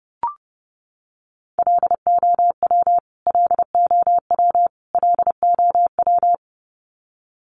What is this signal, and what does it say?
Morse code, heard as a keyed tone, about 30 wpm, 700 Hz. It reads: LOW LOW LOW
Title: Top 500 Words Repeated 30wpm